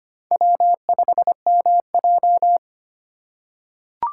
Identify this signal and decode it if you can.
Morse code, heard as a keyed tone, about 25 wpm, 700 Hz. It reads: W5MJ
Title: USA Call Signs - 4 Characters - 4x Character Spacing - 25wpm